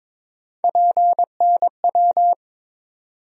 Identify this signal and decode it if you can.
Morse code, heard as a keyed tone, about 22 wpm, 700 Hz. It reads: PNW
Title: USA Call Sign Postfixes - 3 Letters 22wpm